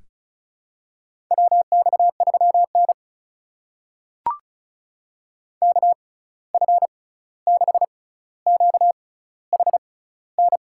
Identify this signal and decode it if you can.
Morse code, heard as a keyed tone, about 36 wpm, spaced as for 16 wpm, 700 Hz. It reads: WX3N KF6QHN
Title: USA Call Signs - 6x Character Spacing - 35wpm